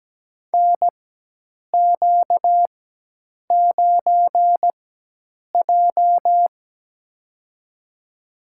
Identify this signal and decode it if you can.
Morse code, heard as a keyed tone, about 17 wpm, spaced as for 8 wpm, 700 Hz. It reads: NQ9J
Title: USA Call Signs - 4 Characters - 4x Character Spacing - 17wpm